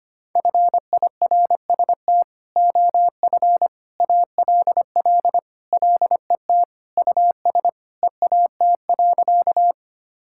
Morse code, heard as a keyed tone, about 25 wpm, 700 Hz. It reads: FIRST OF ALL LET US EAT.
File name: practice / 25wpm / Sentences with 3-Word Phrases - 25wpm